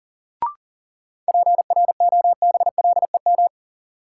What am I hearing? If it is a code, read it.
Morse code, heard as a keyed tone, about 37 wpm, 700 Hz. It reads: PROBLEM
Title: Sentences from Top 1000 Words - New Word Drill - 40wpm